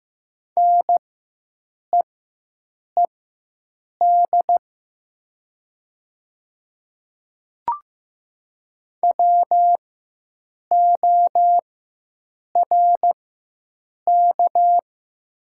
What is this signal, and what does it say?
Morse code, heard as a keyed tone, about 15 wpm, spaced as for 7 wpm, 700 Hz. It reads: NEED WORK
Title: 4x Character Spacing - Top 300 Words - 15wpm